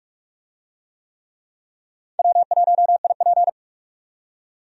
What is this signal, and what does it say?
Morse code, heard as a keyed tone, about 46 wpm, 700 Hz. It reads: W1IP
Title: USA Call Signs - 4 Characters - 45wpm